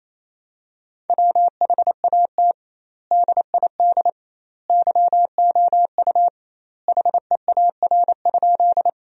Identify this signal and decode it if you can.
Morse code, heard as a keyed tone, about 28 wpm, 700 Hz. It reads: WHAT DID YOU HEAR?
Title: 2x Word-Spacing - Sentences from Top 300 Words - 28wpm